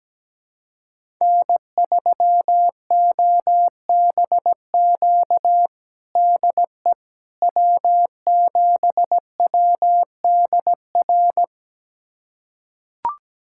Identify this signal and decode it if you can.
Morse code, heard as a keyed tone, about 17 wpm, 700 Hz. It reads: N3OBQ DE W7WDR
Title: USA Callsign DE Callsign - 5 Characters - 17wpm